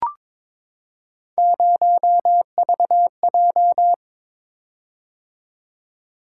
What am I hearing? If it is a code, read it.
Morse code, heard as a keyed tone, about 22 wpm, 700 Hz. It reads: 0VJ
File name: practice / 22wpm / Three Letter-Number Combinations - 22wpm